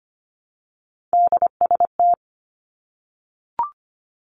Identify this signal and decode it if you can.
Morse code, heard as a keyed tone, about 25 wpm, 700 Hz. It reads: DST
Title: Three Challenging Letter-Number Combinations 25wpm